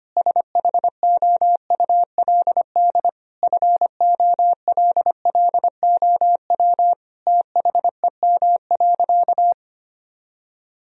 Morse code, heard as a keyed tone, about 25 wpm, 700 Hz. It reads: SHOULD FOLLOW THEM.
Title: Sentences from Top 1000 Words - Review Part 1 - 25wpm